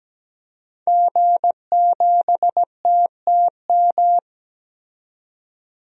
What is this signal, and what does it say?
Morse code, heard as a keyed tone, about 17 wpm, 700 Hz. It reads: G7TTM
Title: DXCC-100 Call Signs - 5 Characters - 17wpm